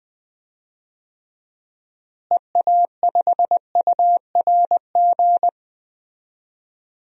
Morse code, heard as a keyed tone, about 20 wpm, 700 Hz. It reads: EA5URG
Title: DXCC-100 Call Signs - 6 Characters - Repeated 3x - 20wpm